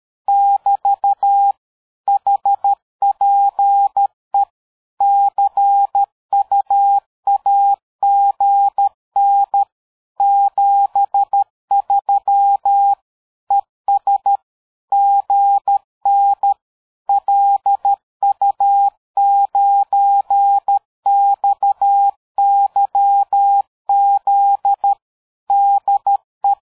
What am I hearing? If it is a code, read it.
Morse code, heard as a keyed tone, about 13 wpm, 800 Hz. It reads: <BT> HPE CUAGN 73 ES GN LU9XYZ DE
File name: dk5ke / QSO 60BpM